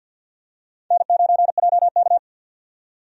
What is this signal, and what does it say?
Morse code, heard as a keyed tone, about 52 wpm, 700 Hz. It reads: N9JK
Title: USA Call Signs - 4 Characters - 50wpm